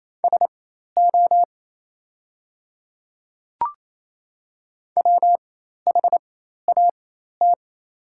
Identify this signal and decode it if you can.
Morse code, heard as a keyed tone, about 27 wpm, spaced as for 13 wpm, 690 Hz. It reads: SO WHAT
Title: 4x Character Spacing - Top 500 Words - 28wpm